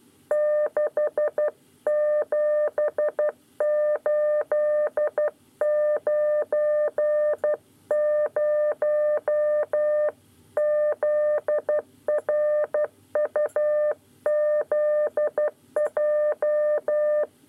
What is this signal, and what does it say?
Morse code, heard as a keyed tone, about 12 wpm, 600 Hz. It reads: 67890 ZRUZJ
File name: 9 lezione cw 2